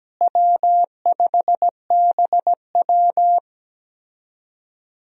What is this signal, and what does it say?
Morse code, heard as a keyed tone, about 17 wpm, 700 Hz. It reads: W5BW
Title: USA Call Signs - 4 Characters - Repeated 2x - 17wpm